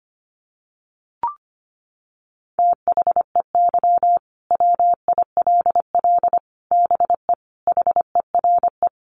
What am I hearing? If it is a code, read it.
Morse code, heard as a keyed tone, about 25 wpm, 700 Hz. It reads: THEY WILL BE HERE
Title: Sentences from Top 600 Words 25wpm